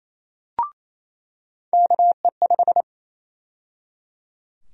Morse code, heard as a keyed tone, about 28 wpm, 700 Hz. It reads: KE5
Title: USA Call Sign Prefixes - 28wpm